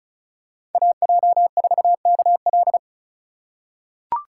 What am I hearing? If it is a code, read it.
Morse code, heard as a keyed tone, about 36 wpm, 700 Hz. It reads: AJ4KL
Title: USA Call Signs - 3x Character Spacing - 35wpm